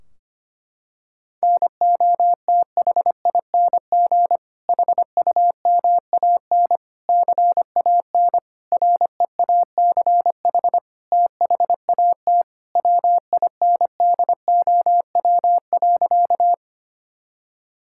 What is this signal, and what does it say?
Morse code, heard as a keyed tone, about 25 wpm, 700 Hz. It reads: NOTHING HUMAN CAN REACH THAT WINDOW.
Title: Sentences from Top 1600 Words 25wpm